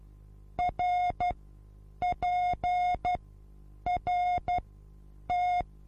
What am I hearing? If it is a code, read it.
Morse code, heard as a keyed tone, about 12 wpm, spaced as for 8 wpm, 720 Hz. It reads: RPRT
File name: repos / Morsekurs / DJ4UF Morsen23